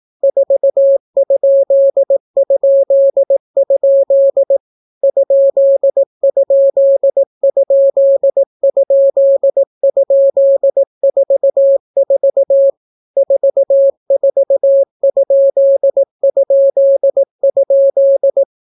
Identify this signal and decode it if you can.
Morse code, heard as a keyed tone, about 18 wpm, 550 Hz. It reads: 4??? ?????44 44???